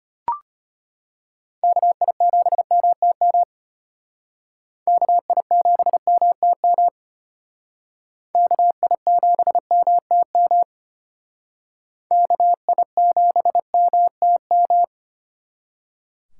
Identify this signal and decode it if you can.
Morse code, heard as a keyed tone, about 32 wpm, 700 Hz. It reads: KI7MTM KI7MTM KI7MTM KI7MTM